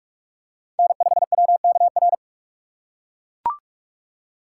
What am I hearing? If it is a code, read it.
Morse code, heard as a keyed tone, about 40 wpm, 700 Hz. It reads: N5WKR